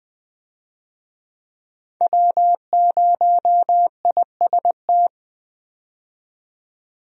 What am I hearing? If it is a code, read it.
Morse code, heard as a keyed tone, about 20 wpm, 700 Hz. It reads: W0IST
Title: USA Call Signs - 5 Characters - 20wpm